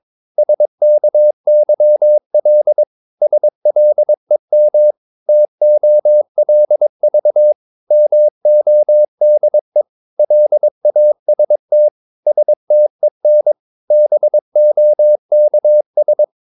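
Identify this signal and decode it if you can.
Morse code, heard as a keyed tone, about 22 wpm, 600 Hz. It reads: SKYL SLEM TOLV MODE LAST STEN BOKS
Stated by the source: Morse practice of short Danish words at 22 wpm, from CW med Gnister.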